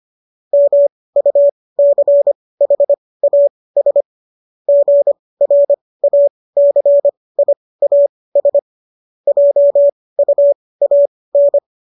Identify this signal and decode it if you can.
Morse code, heard as a keyed tone, about 25 wpm, spaced as for 18 wpm, 570 Hz. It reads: MUCHAS GRACIAS JUAN